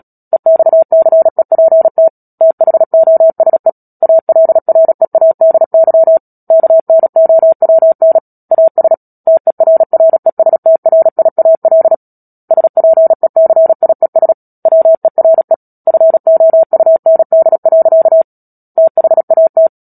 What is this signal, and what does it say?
Morse code, heard as a keyed tone, about 37 wpm, 650 Hz. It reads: EXCEPT THOSE ALREADY KNOWN AS TERRESTRIAL SPECIES WERE FOUND. THAT
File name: Book Two / 10-Chapter 10 - Epilogue - 36 WPM